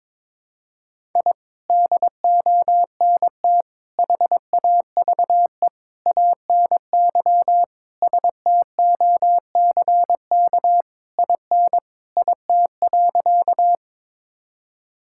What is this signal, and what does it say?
Morse code, heard as a keyed tone, about 22 wpm, 700 Hz. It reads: I DONT HAVE ANY STOCK IN IT.
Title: Sentences from Top 1000 Words - Review Part 5 - 22wpm